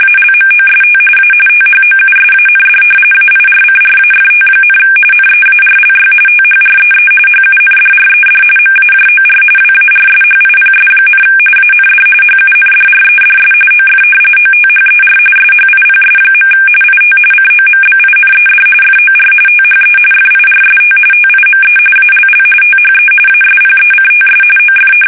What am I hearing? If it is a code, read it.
RTTY, about 45.45 baud. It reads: ALMOST A THOUSAND AIRCRAFT, NEARLY THREE HUNDRED MAJOR WARSHIPS, AND HUNDREDS OF OTHER SHIPS. IT COMBINED FOR THE FIRST TIME ALL THE ASPECTS OF
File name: fsk850 / battleshipnj